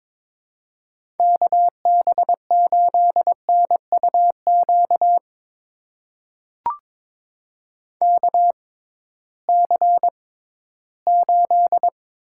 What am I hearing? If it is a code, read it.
Morse code, heard as a keyed tone, about 22 wpm, spaced as for 14 wpm, 700 Hz. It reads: KB8NUQ KC8
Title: USA Call Signs - 6 Characters - 6x Character Spacing - 22wpm